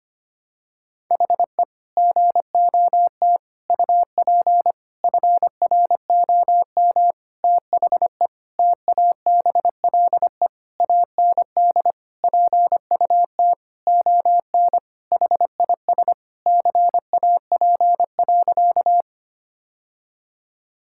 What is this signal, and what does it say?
Morse code, heard as a keyed tone, about 25 wpm, 700 Hz. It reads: HE GOT UP FROM THE TABLE AND PUT ON HIS CAP.
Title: Sentences from Top 1600 Words 25wpm